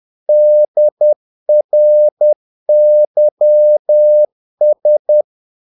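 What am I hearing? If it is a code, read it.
Morse code, heard as a keyed tone, about 10 wpm, 600 Hz. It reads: DRYS
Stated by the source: Morse practice of short Danish words at 10 wpm, from CW med Gnister.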